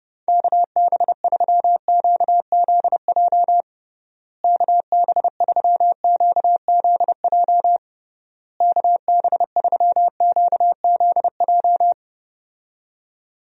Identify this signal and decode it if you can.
Morse code, heard as a keyed tone, about 30 wpm, 700 Hz. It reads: KB3QZJ KB3QZJ KB3QZJ
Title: USA Call Signs - 6 Characters - Repeated 3x - 30wpm